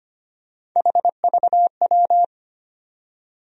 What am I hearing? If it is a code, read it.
Morse code, heard as a keyed tone, about 25 wpm, 700 Hz. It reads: HVW
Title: USA Call Sign Postfixes - 3 Letters 25wpm